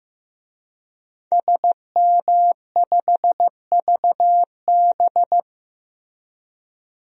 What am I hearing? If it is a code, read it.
Morse code, heard as a keyed tone, about 15 wpm, 700 Hz. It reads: SM5VB
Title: DXCC-100 Call Signs - 5 Characters - Repeated 3x - 15wpm